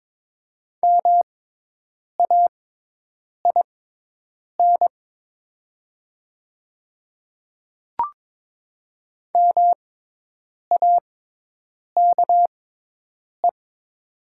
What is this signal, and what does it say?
Morse code, heard as a keyed tone, about 22 wpm, spaced as for 8 wpm, 700 Hz. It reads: MAIN MAKE
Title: 6x Character Spacing - Top 500 Words - 22wpm